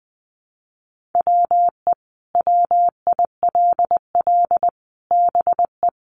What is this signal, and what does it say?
Morse code, heard as a keyed tone, about 20 wpm, 700 Hz. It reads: WE WILL BE
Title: Sentences from Top 600 Words 20wpm